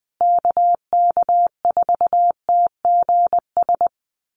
Morse code, heard as a keyed tone, about 20 wpm, 700 Hz. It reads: KK4TGS
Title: USA Call Signs Repeated 3x 20wpm